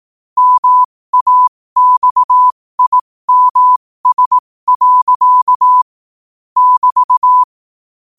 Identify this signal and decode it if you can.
Morse code, heard as a keyed tone, about 18 wpm, spaced as for 15 wpm, 1000 Hz. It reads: MAXIMS. <BT>
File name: qotdfiles / 15wpm